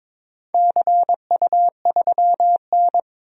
Morse code, heard as a keyed tone, about 22 wpm, 700 Hz. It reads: CU3N
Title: DXCC-100 Call Signs - 4 Characters - 22wpm